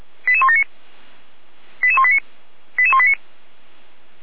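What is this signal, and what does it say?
Five-tone selective call (ZVEI), heard as quick runs of five tones, tones of about 80 ms. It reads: 88179 88189 88189
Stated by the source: ZVEI five-tone call received on SDR, from audio server of feuerwehr-orlamuende (fire brigade Orlamünde).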